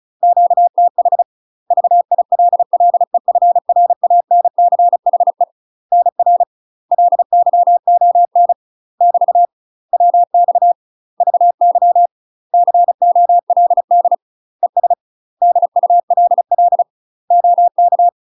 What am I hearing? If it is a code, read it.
Morse code, heard as a keyed tone, about 36 wpm, 700 Hz. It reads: QTH VILLEFRANCHE NR LYON <BT> WX VY COLD ES DULL OK
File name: UK MASTER QSOs TRAINING/35WPM